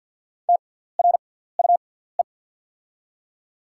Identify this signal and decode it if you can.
Morse code, heard as a keyed tone, about 48 wpm, spaced as for 17 wpm, 690 Hz. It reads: TRUE
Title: 6x Character Spacing - Top 500 Words - 50wpm